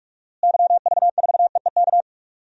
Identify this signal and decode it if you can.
Morse code, heard as a keyed tone, about 46 wpm, 700 Hz. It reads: YV4EEK
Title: DXCC-100 Call Signs - 6 Characters - Repeated 2x - 45wpm